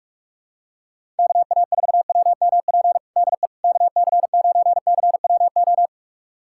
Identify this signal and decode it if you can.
Morse code, heard as a keyed tone, about 46 wpm, 700 Hz. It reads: KA4WMP DE KC9CWO
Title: USA Callsign DE Callsign - 6 Characters - 45wpm